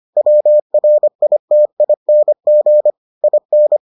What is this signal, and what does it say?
Morse code, heard as a keyed tone, about 25 wpm, 600 Hz. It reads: WRITING IN